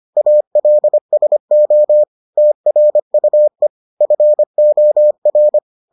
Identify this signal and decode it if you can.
Morse code, heard as a keyed tone, about 25 wpm, 600 Hz. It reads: ALSO TRUE FOR